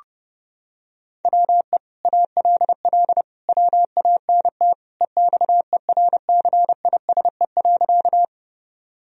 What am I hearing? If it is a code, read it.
Morse code, heard as a keyed tone, about 30 wpm, 700 Hz. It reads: WE ALL WANT EXERCISE.
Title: Sentences from Top 1000 Words - Review Part 4 - 30wpm